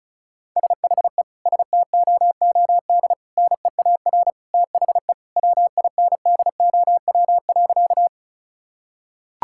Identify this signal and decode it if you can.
Morse code, heard as a keyed tone, about 36 wpm, 700 Hz. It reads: SHE STOOD NEAR THE WINDOW.
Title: Sentences from Top 1600 Words 35wpm